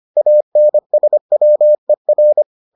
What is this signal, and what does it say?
Morse code, heard as a keyed tone, about 25 wpm, 600 Hz. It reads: ANSWER